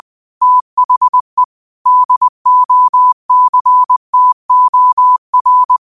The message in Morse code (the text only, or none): THE DOCTOR